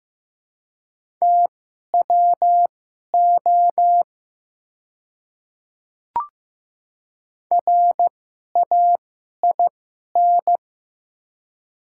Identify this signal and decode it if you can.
Morse code, heard as a keyed tone, about 15 wpm, spaced as for 11 wpm, 700 Hz. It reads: TWO RAIN
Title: 2x Character Spacing - Top 300 Words - 15wpm